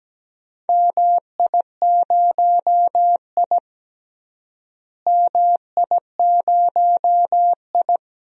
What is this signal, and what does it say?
Morse code, heard as a keyed tone, about 17 wpm, 700 Hz. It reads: MI0I MI0I
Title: DXCC-100 Call Signs - 4 Characters - Repeated 2x - 17wpm